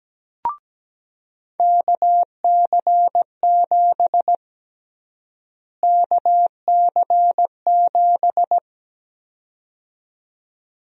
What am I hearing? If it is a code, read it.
Morse code, heard as a keyed tone, about 17 wpm, 700 Hz. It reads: KC7 KC7
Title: USA Call Sign Prefixes - Repeated 2x - 17wpm